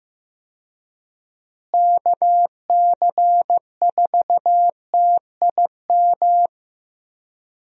Morse code, heard as a keyed tone, about 15 wpm, 700 Hz. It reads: KC4TIM
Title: USA Call Signs - 6 Characters - Repeated 3x - 15wpm